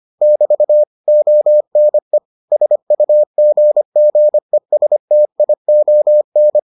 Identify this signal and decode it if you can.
Morse code, heard as a keyed tone, about 25 wpm, 600 Hz. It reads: <BT> ONE SUGGESTION